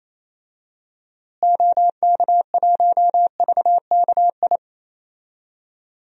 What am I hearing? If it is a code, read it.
Morse code, heard as a keyed tone, about 28 wpm, 700 Hz. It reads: OK1VKI
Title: DXCC-100 Call Signs - 6 Characters - Repeated 2x - 28wpm